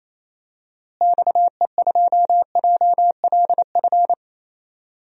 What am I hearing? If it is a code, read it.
Morse code, heard as a keyed tone, about 28 wpm, 700 Hz. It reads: XE2JLF
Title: DXCC-100 Call Signs - 6 Characters - Repeated 3x - 28wpm